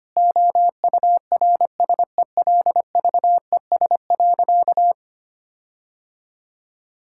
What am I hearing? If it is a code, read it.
Morse code, heard as a keyed tone, about 25 wpm, 700 Hz. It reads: OURSELVES.
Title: Sentences from Top 1600 Words 25wpm